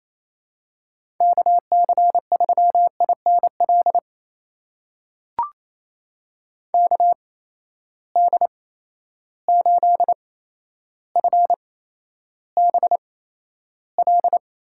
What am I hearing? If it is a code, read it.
Morse code, heard as a keyed tone, about 28 wpm, spaced as for 12 wpm, 700 Hz. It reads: KC3INL KD8FBL
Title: USA Call Signs - 6 Characters - 8x Character Spacing - 28wpm